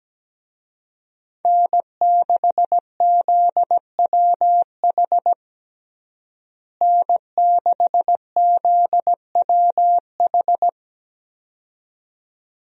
Morse code, heard as a keyed tone, about 17 wpm, 700 Hz. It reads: N6ZWH N6ZWH
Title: USA Call Signs - 5 Characters - Repeated 2x - 17wpm